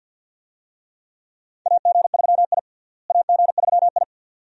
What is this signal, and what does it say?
Morse code, heard as a keyed tone, about 53 wpm, 700 Hz. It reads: AG3I AG3I
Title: USA Call Signs - 4 Characters - Repeated 2x - 50wpm